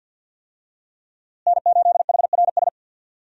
Morse code, heard as a keyed tone, about 53 wpm, 700 Hz. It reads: N8HRS
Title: USA Call Signs - 5 Characters - Repeated 3x - 50wpm